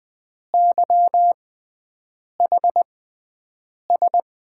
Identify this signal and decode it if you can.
Morse code, heard as a keyed tone, about 20 wpm, spaced as for 7 wpm, 700 Hz. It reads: YHS
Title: USA Call Signs - 6x Character Spacing - 20wpm